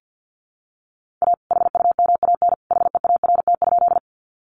Morse code, heard as a keyed tone, about 50 wpm, 700 Hz. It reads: A HUMAN HEART?
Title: Sentences from Top 1200 Words 50wpm